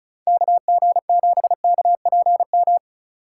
Morse code, heard as a keyed tone, about 36 wpm, 700 Hz. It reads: KG7KPM
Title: USA Call Signs - 6 Characters - 4x Character Spacing - 35wpm